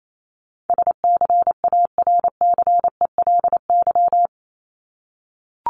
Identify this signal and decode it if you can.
Morse code, heard as a keyed tone, about 28 wpm, 700 Hz. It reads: SCARCELY
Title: LY Suffix - Top 100 Words with Base Words 28wpm